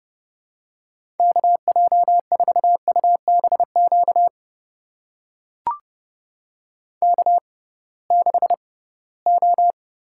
Morse code, heard as a keyed tone, about 30 wpm, spaced as for 19 wpm, 700 Hz. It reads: KJ4UBQ K6O
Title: USA Call Signs - 6x Character Spacing - 30wpm